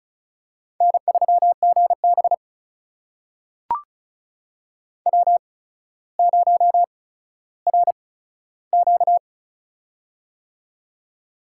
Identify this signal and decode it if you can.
Morse code, heard as a keyed tone, about 36 wpm, spaced as for 15 wpm, 700 Hz. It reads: N3GB W0RQ
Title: USA Call Signs - 4 Characters - 8x Character Spacing - 35wpm